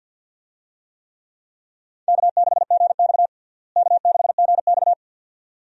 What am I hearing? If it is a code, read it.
Morse code, heard as a keyed tone, about 52 wpm, 700 Hz. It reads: K6GX K6GX
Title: USA Call Signs - 4 Characters - Repeated 2x - 50wpm